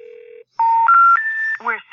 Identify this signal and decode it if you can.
Special information tone, with speech.